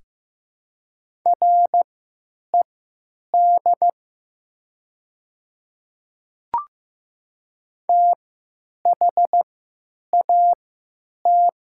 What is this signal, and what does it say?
Morse code, heard as a keyed tone, about 15 wpm, spaced as for 9 wpm, 700 Hz. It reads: RED THAT
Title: 3x Character Spacing - Top 500 Words - 15wpm